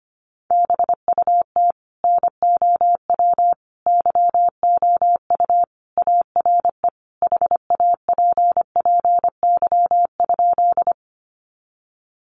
Morse code, heard as a keyed tone, about 25 wpm, 700 Hz. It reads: BUT NOW YOU ARE HAPPY?
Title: Sentences from Top 1000 Words 25wpm